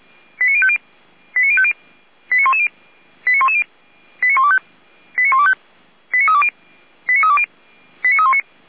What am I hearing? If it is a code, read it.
Five-tone selective call (ZVEI), heard as quick runs of five tones, tones of about 70 ms. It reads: 89050 89110 89110 89125 89125 89320 89320 89319